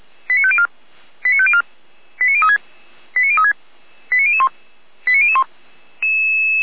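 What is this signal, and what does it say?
Five-tone selective call (ZVEI), heard as quick runs of five tones, tones of about 70 ms. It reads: 89504 89504 89036 89036 89001 89001
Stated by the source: ZVEI five-tone selective call received with an SDR, recorded by feuerwehr-orlamuende fire brigade.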